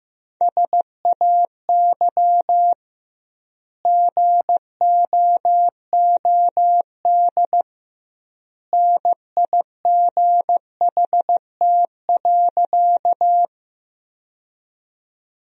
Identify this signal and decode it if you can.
Morse code, heard as a keyed tone, about 15 wpm, 700 Hz. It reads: SAY GOOD NIGHT.
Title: 2x Word-Spacing - Sentences from Top 300 Words - 15wpm